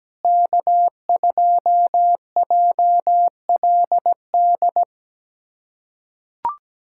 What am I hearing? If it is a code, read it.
Morse code, heard as a keyed tone, about 17 wpm, 700 Hz. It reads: K2JLD
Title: USA Callsign DE Callsign - 5 Characters - 17wpm